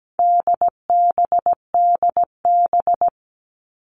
Morse code, heard as a keyed tone, about 17 wpm, 700 Hz. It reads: DBDB
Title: All Cut Numbers - 4-Digit 17wpm